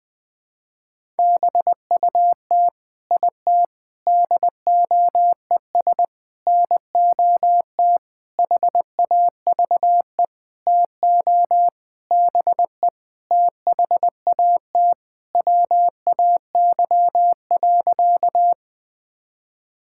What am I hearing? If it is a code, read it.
Morse code, heard as a keyed tone, about 20 wpm, 700 Hz. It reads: BUT IT DOES NOT HAVE TO BE THAT WAY.
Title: Sentences with 8-Word Phrases - 20wpm